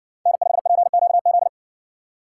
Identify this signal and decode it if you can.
Morse code, heard as a keyed tone, about 60 wpm, 700 Hz. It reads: N5PQZ